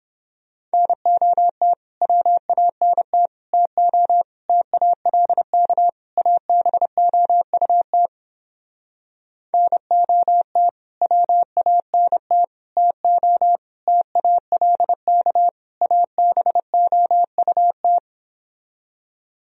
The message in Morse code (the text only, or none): NOT WANT TO TALK ABOUT NOT WANT TO TALK ABOUT